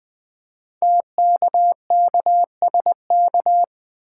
Morse code, heard as a keyed tone, about 20 wpm, 700 Hz. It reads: TKKSK
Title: Code Groups - 5 Letters - 20wpm